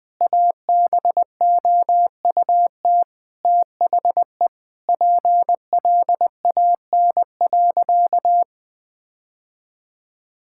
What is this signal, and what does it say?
Morse code, heard as a keyed tone, about 21 wpm, 700 Hz. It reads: ABOUT THE PLAN.
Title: Sentences with 4-Word Phrases - Spoken Beginning - 20wpm